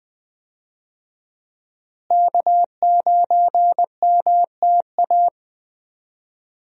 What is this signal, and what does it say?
Morse code, heard as a keyed tone, about 20 wpm, 700 Hz. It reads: K9MTA
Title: USA Call Signs - 5 Characters - Repeated 2x - 20wpm